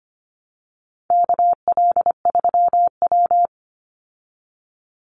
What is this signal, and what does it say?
Morse code, heard as a keyed tone, about 25 wpm, 700 Hz. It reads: KL3W
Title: Call Signs Repeated 2x - 25wpm